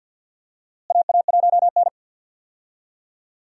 Morse code, heard as a keyed tone, about 49 wpm, 700 Hz. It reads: AA1N
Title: USA Call Signs - 4 Characters - Repeated 3x - 50wpm